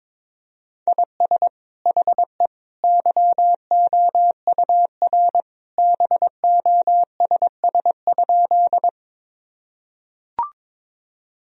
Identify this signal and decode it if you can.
Morse code, heard as a keyed tone, about 22 wpm, 700 Hz. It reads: IS HE YOUR BOSS?
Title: Sentences from Top 1000 Words - Review Part 4 - 22wpm